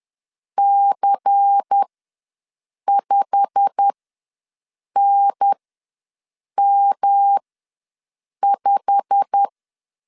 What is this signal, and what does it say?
Morse code, heard as a keyed tone, about 10 wpm, spaced as for 6 wpm, 800 Hz. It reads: C5NM5
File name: LEZIONI / 8 YJ45 / 8 YJ45 ws 10wpm 3843